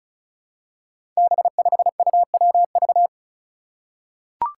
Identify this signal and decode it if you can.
Morse code, heard as a keyed tone, about 34 wpm, 700 Hz. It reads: B5UWV
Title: Five Challenging Letter-Number Combinations 35wpm